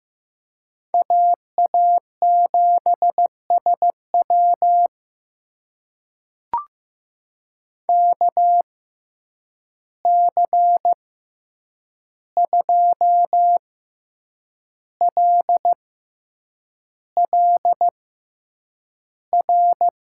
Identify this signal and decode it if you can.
Morse code, heard as a keyed tone, about 15 wpm, spaced as for 7 wpm, 700 Hz. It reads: AA7SW KC2LLR
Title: USA Call Signs - 6x Character Spacing - 15wpm